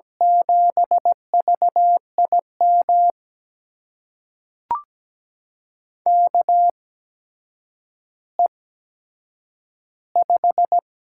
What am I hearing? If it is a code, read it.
Morse code, heard as a keyed tone, about 17 wpm, spaced as for 8 wpm, 700 Hz. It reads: 7VIM KE5
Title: USA Call Signs - 6 Characters - 8x Character Spacing - 17wpm